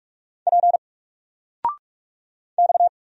Morse code, heard as a keyed tone, about 44 wpm, 700 Hz. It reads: P X
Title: Single Letter-Number 45wpm